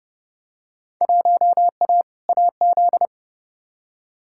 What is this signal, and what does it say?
Morse code, heard as a keyed tone, about 32 wpm, 700 Hz. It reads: 1A AZ
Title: Field Day Exchanges - 30wpm